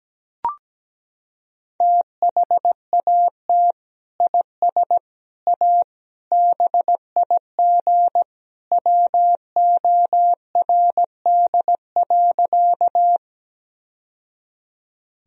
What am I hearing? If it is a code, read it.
Morse code, heard as a keyed tone, about 17 wpm, 700 Hz. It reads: THAT IS A BIG WORD.
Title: Sentences from Top 1000 Words - Review Part 1 - 17wpm